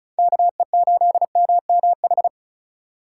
Morse code, heard as a keyed tone, about 34 wpm, 700 Hz. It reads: KE8MMH
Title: USA Call Signs - 6 Characters - 3x Character Spacing - 35wpm